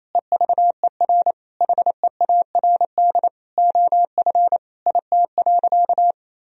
Morse code, heard as a keyed tone, about 28 wpm, 700 Hz. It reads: EVER HEARD OF IT.
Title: Sentences from Top 1000 Words - Review Part 3 - 28wpm